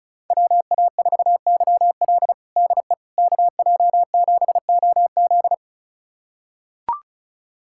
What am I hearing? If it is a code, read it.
Morse code, heard as a keyed tone, about 36 wpm, 700 Hz. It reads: WA4YL DE KJ7OZ
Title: USA Callsign DE Callsign 35wpm